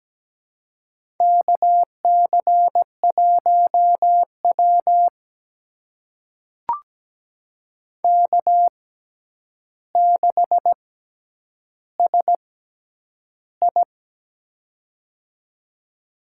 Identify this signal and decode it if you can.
Morse code, heard as a keyed tone, about 17 wpm, spaced as for 9 wpm, 700 Hz. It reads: KC1W K6SI
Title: USA Call Signs - 4 Characters - 6x Character Spacing - 17wpm